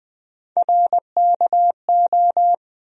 Morse code, heard as a keyed tone, about 20 wpm, 700 Hz. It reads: RKO
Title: USA Call Sign Postfixes - 3 Letters 20wpm